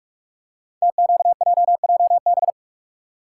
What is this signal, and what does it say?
Morse code, heard as a keyed tone, about 45 wpm, 700 Hz. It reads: TQJJB
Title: Code Groups - 5 Letter:Numbers - 45wpm